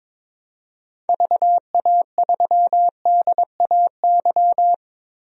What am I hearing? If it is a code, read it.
Morse code, heard as a keyed tone, about 22 wpm, 700 Hz. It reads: VA3DAY
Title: DXCC-100 Call Signs - 6 Characters - Repeated 3x - 22wpm